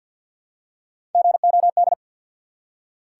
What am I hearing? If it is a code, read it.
Morse code, heard as a keyed tone, about 53 wpm, 700 Hz. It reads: GOD
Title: Three Letter Words - 50wpm